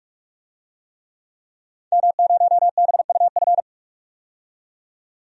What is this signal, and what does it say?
Morse code, heard as a keyed tone, about 43 wpm, 700 Hz. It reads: M0BUF
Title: DXCC-100 Call Signs - 5 Characters - 45wpm